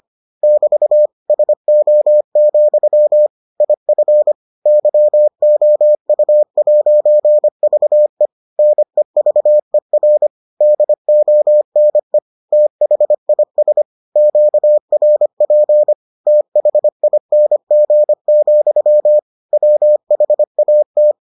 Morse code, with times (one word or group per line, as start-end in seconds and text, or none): 0.4-1.1 <BT>
1.3-3.3 SO,
3.6-4.3 IF
4.7-8.3 YOU'VE
8.6-10.3 NEVER
10.6-12.2 DONE
12.5-13.8 THIS
14.2-15.9 QRP
16.3-19.2 THING,
19.5-21.1 WHAT